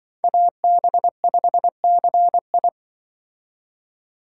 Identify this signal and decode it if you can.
Morse code, heard as a keyed tone, about 24 wpm, 700 Hz. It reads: AB5CI
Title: Speed Racing - USA 2x2 Call Signs 20wpm